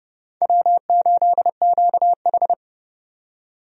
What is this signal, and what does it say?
Morse code, heard as a keyed tone, about 30 wpm, 700 Hz. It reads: W8QH
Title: USA Call Signs - 4 Characters - 30wpm